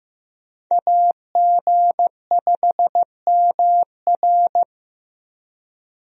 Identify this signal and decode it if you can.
Morse code, heard as a keyed tone, about 15 wpm, 700 Hz. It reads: AG5MR
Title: USA Call Signs - 5 Characters - Repeated 3x - 15wpm